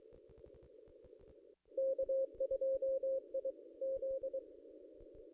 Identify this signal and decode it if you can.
Morse code, heard as a keyed tone, about 24 wpm, 530 Hz. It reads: K2IZ
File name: contests / ss-cw-k5tr-op-n5rz -1800